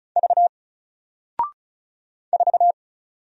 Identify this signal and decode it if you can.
Morse code, heard as a keyed tone, about 36 wpm, 700 Hz. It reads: V 4
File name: Single-Letter-Number-35wpm